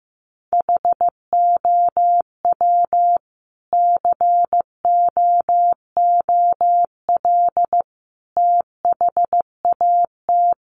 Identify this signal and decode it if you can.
Morse code, heard as a keyed tone, about 15 wpm, 700 Hz. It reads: HOW COOL THAT
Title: Sentences from Top 1200 Words 15wpm